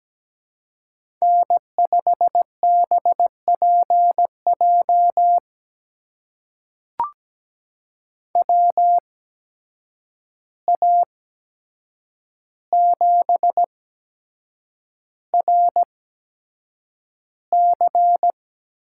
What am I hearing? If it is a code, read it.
Morse code, heard as a keyed tone, about 17 wpm, spaced as for 7 wpm, 700 Hz. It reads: N5BPJ WA7RC
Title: USA Call Signs - 5 Characters - 8x Character Spacing - 17wpm